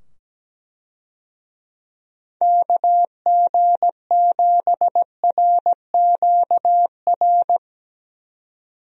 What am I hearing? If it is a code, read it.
Morse code, heard as a keyed tone, about 17 wpm, 700 Hz. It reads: KG7RQR
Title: USA Call Signs - 6 Characters - 17wpm